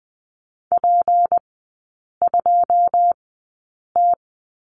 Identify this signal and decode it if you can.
Morse code, heard as a keyed tone, about 20 wpm, spaced as for 8 wpm, 700 Hz. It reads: P2T
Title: 152 Lesson 25e - Letter p - Three Characters 20wpm